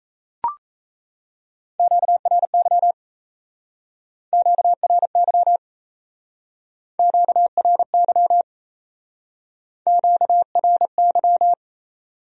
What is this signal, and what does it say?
Morse code, heard as a keyed tone, about 35 wpm, 700 Hz. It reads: QRY QRY QRY QRY
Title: Speed-Racing Q-Codes - 28wpm